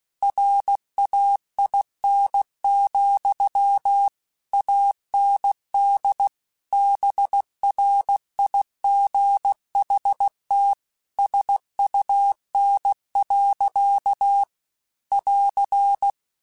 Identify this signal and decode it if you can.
Morse code, heard as a keyed tone, about 16 wpm, 800 Hz. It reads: RAIN, AND BRIGHT SUN. <AR>
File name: Weather Sets / Weather Sets-mp3s / Weather 5 5 @15